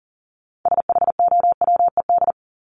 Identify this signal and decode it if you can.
Morse code, heard as a keyed tone, about 40 wpm, 700 Hz. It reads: SHOWED
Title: ED Suffix - Top 100 Words with Base Words 40wpm